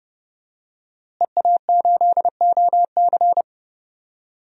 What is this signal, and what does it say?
Morse code, heard as a keyed tone, about 30 wpm, 700 Hz. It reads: EA8OC
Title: DXCC-100 Call Signs - 5 Characters - Repeated 3x - 30wpm